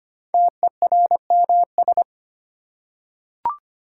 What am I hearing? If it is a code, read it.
Morse code, heard as a keyed tone, about 25 wpm, 700 Hz. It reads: TERMS
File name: 5 Letter Words - First 3 Letters Given - 25wpm